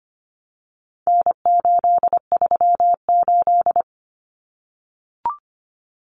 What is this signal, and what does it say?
Morse code, heard as a keyed tone, about 25 wpm, 700 Hz. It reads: N838
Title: Common Cut Numbers - 4-Digit 25wpm